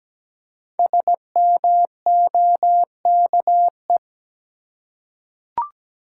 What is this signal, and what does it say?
Morse code, heard as a keyed tone, about 17 wpm, 700 Hz. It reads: SMOKE
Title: Five Letter Words - 17wpm